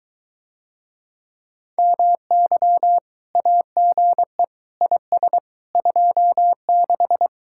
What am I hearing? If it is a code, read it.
Morse code, heard as a keyed tone, about 23 wpm, 700 Hz. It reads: MY AGE IS 26